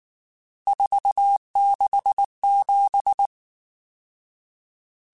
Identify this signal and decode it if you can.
Morse code, heard as a keyed tone, about 19 wpm, 800 Hz. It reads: 467
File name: RSTs / RST-mp3s / Various RST @18